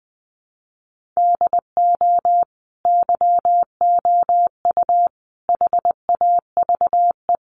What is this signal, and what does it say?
Morse code, heard as a keyed tone, about 20 wpm, 700 Hz. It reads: DO YOU HAVE
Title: Sentences from Top 600 Words 20wpm